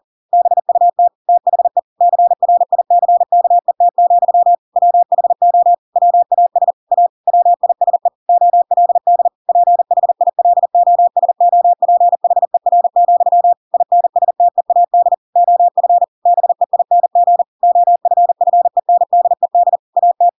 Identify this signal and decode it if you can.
Morse code, heard as a keyed tone, about 40 wpm, 700 Hz. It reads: BUT THE CRICKET, WHO WAS A WISE OLD PHILOSOPHER, INSTEAD OF BEING OFFENDED AT
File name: Pinocchio / Pinocchio ch4 40 40 ik1hge